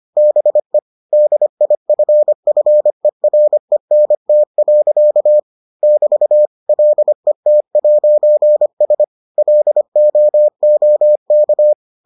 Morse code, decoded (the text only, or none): BE DIFFERENT. <BT> LET'S LOOK